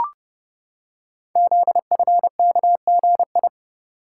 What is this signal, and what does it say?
Morse code, heard as a keyed tone, about 30 wpm, 700 Hz. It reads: ZFKGI
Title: Code Groups - 5 Letters - 30wpm